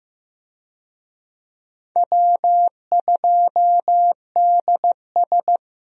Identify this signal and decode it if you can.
Morse code, heard as a keyed tone, about 15 wpm, 700 Hz. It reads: W2DS
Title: USA Call Signs - 4 Characters - Repeated 3x - 15wpm